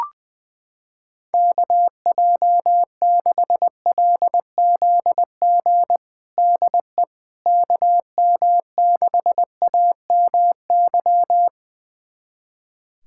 Morse code, heard as a keyed tone, about 20 wpm, 700 Hz. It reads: KJ6LZG DE KM6AMY
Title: USA Callsign DE Callsign - 6 Characters - 20wpm